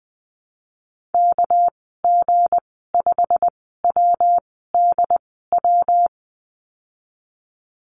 Morse code, heard as a keyed tone, about 20 wpm, spaced as for 14 wpm, 700 Hz. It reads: KG5WDW
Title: 238 Lesson 38g - Letter q - Call Signs 20wpm